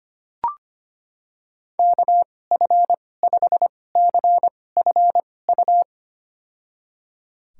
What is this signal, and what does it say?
Morse code, heard as a keyed tone, about 25 wpm, spaced as for 18 wpm, 700 Hz. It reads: KF5CFU
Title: USA Call Signs - 6 Characters - 2x Character Spacing - 25wpm